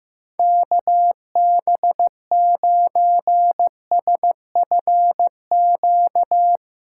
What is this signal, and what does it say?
Morse code, heard as a keyed tone, about 16 wpm, 700 Hz. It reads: KB9SFQ
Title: USA Callsign DE Callsign - 6 Characters - 15wpm